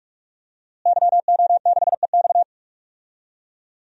Morse code, heard as a keyed tone, about 46 wpm, 700 Hz. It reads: YO6EX
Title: DXCC-100 Call Signs - 5 Characters - Repeated 3x - 45wpm